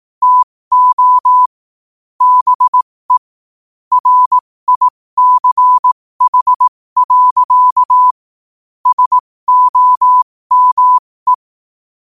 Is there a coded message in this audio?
Yes: Morse code from 0.2 to 11.3 s, heard as a keyed tone, about 18 wpm, spaced as for 15 wpm, 1000 Hz. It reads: TO BE RICH. SOME